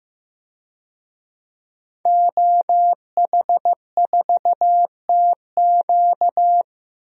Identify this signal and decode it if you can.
Morse code, heard as a keyed tone, about 15 wpm, 700 Hz. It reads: OH4TQ
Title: DXCC-100 Call Signs - 5 Characters - 15wpm